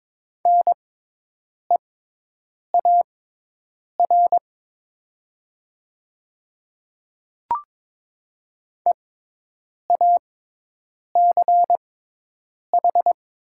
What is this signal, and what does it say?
Morse code, heard as a keyed tone, about 22 wpm, spaced as for 8 wpm, 700 Hz. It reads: NEAR EACH
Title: 6x Character Spacing - Top 200 Words - 22wpm